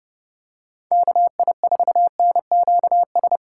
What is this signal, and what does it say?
Morse code, heard as a keyed tone, about 30 wpm, 700 Hz. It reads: KI4NQS
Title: USA Call Signs - 6x Character Spacing - 30wpm